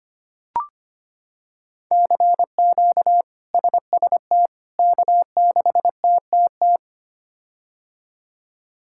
Morse code, heard as a keyed tone, about 25 wpm, 700 Hz. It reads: CQ SST K6TTT
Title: K1USN SST QSO Practice - 25wpm